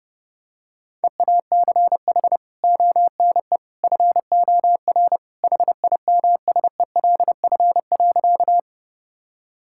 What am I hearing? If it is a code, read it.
Morse code, heard as a keyed tone, about 30 wpm, 700 Hz. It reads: EACH ONE FOR HIMSELF.
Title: Sentences from Top 1600 Words 30wpm